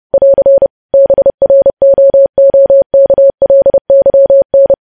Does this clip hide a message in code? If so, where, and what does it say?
Yes: Morse code from 0.1 to 4.7 s, heard as a keyed tone, about 30 wpm, 550 Hz. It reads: <AR> BROOKLYN